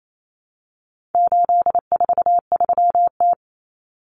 Morse code, heard as a keyed tone, about 28 wpm, 700 Hz. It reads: 843T
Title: Common Cut Numbers - 4-Digit 28wpm